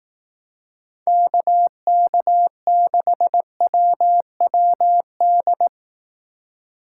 Morse code, heard as a keyed tone, about 18 wpm, 700 Hz. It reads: KK6WWD
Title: Speed Racing - USA 2x3 Call Signs 15wpm